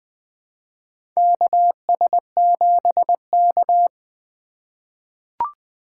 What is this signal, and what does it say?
Morse code, heard as a keyed tone, about 20 wpm, 700 Hz. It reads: KS7K
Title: USA Call Signs - 4 Characters - 2x Character Spacing - 20wpm